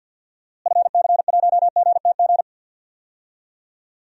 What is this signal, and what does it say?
Morse code, heard as a keyed tone, about 50 wpm, 700 Hz. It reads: FC1GTG
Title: DXCC-100 Call Signs - 6 Characters - Repeated 3x - 50wpm